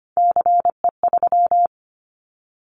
Morse code, heard as a keyed tone, about 25 wpm, 700 Hz. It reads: CE3
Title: Callsign Prefixes for Top 100 DXCC - 25wpm